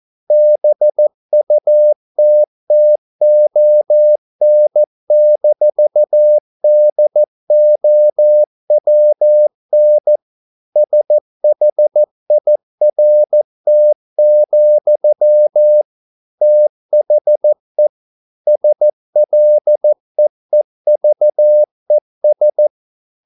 Morse code, heard as a keyed tone, about 14 wpm, 600 Hz. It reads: BUTTON-DOWN SHIRT, THE SLEEVES